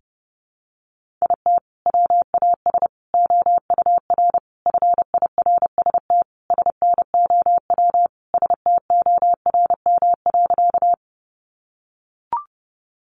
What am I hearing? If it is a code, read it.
Morse code, heard as a keyed tone, about 30 wpm, 700 Hz. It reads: IT WAS OUR FIRST SNOW STORM.
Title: Sentences from Top 1500 Words 30wpm